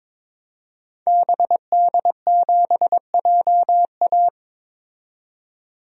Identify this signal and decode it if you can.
Morse code, heard as a keyed tone, about 21 wpm, 700 Hz. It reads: BD7JA
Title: DXCC-100 Call Signs - 5 Characters - 22wpm